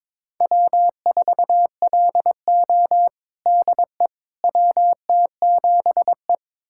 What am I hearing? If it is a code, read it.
Morse code, heard as a keyed tone, about 21 wpm, 700 Hz. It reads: W4LO DE WT7E
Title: USA Callsign DE Callsign - 4 Characters - 22wpm